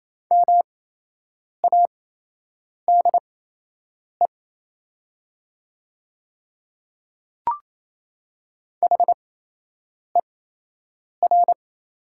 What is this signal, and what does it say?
Morse code, heard as a keyed tone, about 28 wpm, spaced as for 8 wpm, 700 Hz. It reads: MADE HER
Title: 8x Character Spacing - Top 200 Words - 28wpm